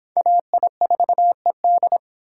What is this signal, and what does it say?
Morse code, heard as a keyed tone, about 31 wpm, 700 Hz. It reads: AI4ED